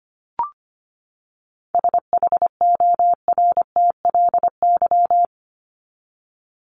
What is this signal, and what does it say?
Morse code, heard as a keyed tone, about 25 wpm, 700 Hz. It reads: SHORTLY
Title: LY Suffix - Top 100 Words with Base Words 25wpm